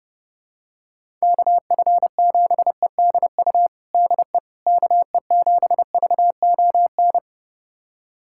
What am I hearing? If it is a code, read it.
Morse code, heard as a keyed tone, about 32 wpm, 700 Hz. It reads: KF7EDU DE KE7VON
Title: USA Callsign DE Callsign - 6 Characters - 30wpm